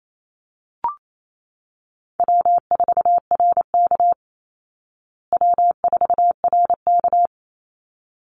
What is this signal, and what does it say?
Morse code, heard as a keyed tone, about 28 wpm, 700 Hz. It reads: W4RK W4RK
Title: Easy Call Signs Repeated 2x - 28wpm